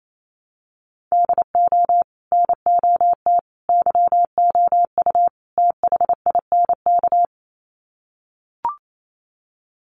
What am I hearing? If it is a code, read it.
Morse code, heard as a keyed tone, about 28 wpm, 700 Hz. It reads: DO NOT YOU THINK
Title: Sets of 4 Words 28wpm